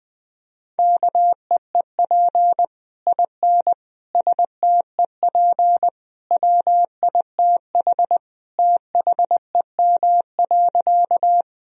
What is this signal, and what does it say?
Morse code, heard as a keyed tone, about 20 wpm, 700 Hz. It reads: KEEP IN STEP WITH THEM.
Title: Sentences from Top 1000 Words - Review Part 2 - 20wpm